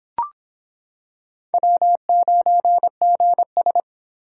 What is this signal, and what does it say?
Morse code, heard as a keyed tone, about 26 wpm, 700 Hz. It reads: W9GS